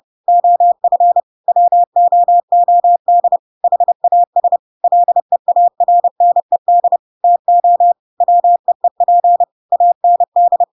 Morse code, heard as a keyed tone, about 30 wpm, 700 Hz. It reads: OF WOOD HAS LEARNED TO WEEP AND
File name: Pinocchio / Pinocchio ch1 30 30 ik1hge